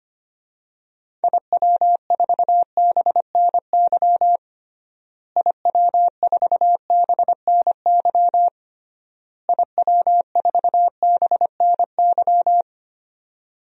Morse code, heard as a keyed tone, about 25 wpm, 700 Hz. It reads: IW4BNY IW4BNY IW4BNY
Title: DXCC-100 Call Signs - 6 Characters - Repeated 3x - 25wpm